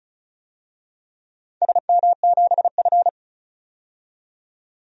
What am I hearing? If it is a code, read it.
Morse code, heard as a keyed tone, about 36 wpm, 700 Hz. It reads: SM7F
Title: DXCC-100 Call Signs - 4 Characters - 35wpm